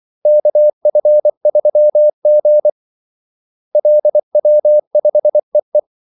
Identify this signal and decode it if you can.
Morse code, heard as a keyed tone, about 24 wpm, 600 Hz. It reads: KF3G LW5EE
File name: Callsigns-24wpm